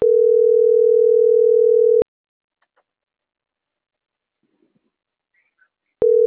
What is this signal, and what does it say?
Ringback tone (North America), 440 + 480 Hz.